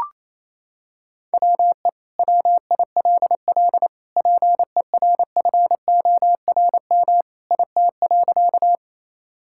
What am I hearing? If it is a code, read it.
Morse code, heard as a keyed tone, about 28 wpm, 700 Hz. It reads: WE WILL PERFORM IT.
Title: Sentences from Top 1000 Words - Review Part 5 - 28wpm